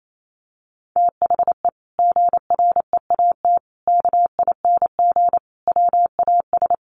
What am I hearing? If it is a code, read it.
Morse code, heard as a keyed tone, about 28 wpm, 700 Hz. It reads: THE GREAT KING WAS
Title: Sentences from Top 1500 Words 28wpm